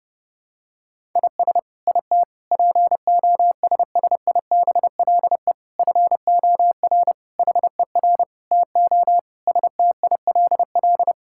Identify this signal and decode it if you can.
Morse code, heard as a keyed tone, about 28 wpm, 700 Hz. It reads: IS IT POSSIBLE FOR HER TO STILL
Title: Sentences from Top 1000 Words - Review Part 6 - 30wpm